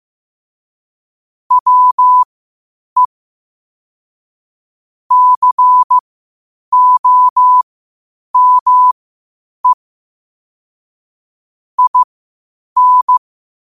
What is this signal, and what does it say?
Morse code, heard as a keyed tone, about 15 wpm, spaced as for 8 wpm, 1000 Hz. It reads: WE COME IN
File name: qotdfiles / 8wpm